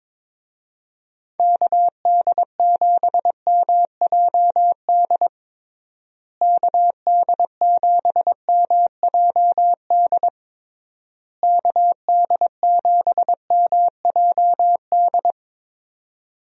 Morse code, heard as a keyed tone, about 22 wpm, 700 Hz. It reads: KD7MJD KD7MJD KD7MJD
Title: USA Call Signs - 6 Characters - Repeated 3x - 22wpm